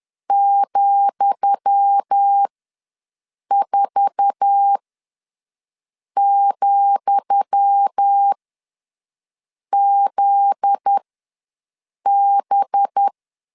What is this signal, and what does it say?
Morse code, heard as a keyed tone, about 10 wpm, spaced as for 5 wpm, 800 Hz. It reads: ,4,ZB
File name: LEZIONI / 10 SIGNS / 10D COMA ws 10wpm 4314